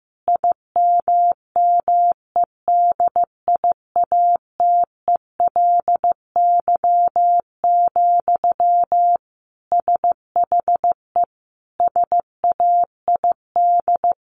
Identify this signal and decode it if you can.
Morse code, heard as a keyed tone, about 15 wpm, 700 Hz. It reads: IMMEDIATELY, SHE SAID
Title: Sentences from Top 1500 Words 15wpm